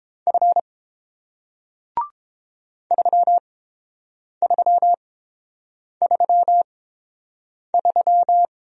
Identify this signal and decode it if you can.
Morse code, heard as a keyed tone, about 30 wpm, spaced as for 7 wpm, 700 Hz. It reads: F 3333